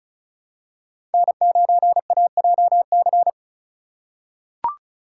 Morse code, heard as a keyed tone, about 36 wpm, 700 Hz. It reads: N9AJC
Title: USA Call Signs - 3x Character Spacing - 35wpm